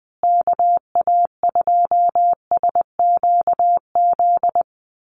Morse code, heard as a keyed tone, about 20 wpm, 700 Hz. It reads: KA2SQZ
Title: USA Call Signs Repeated 2x 20wpm